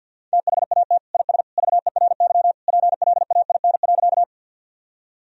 Morse code, heard as a keyed tone, about 52 wpm, 700 Hz. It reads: THAT IS VERY PLAIN.
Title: Sentences from Top 1000 Words - Review Part 2 - 50wpm